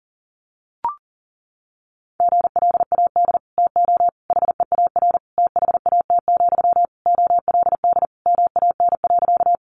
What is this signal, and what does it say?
Morse code, heard as a keyed tone, about 40 wpm, 700 Hz. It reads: GLAD TO HEAR THAT, OLD MAN.
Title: Sentences from Top 800 Words 40wpm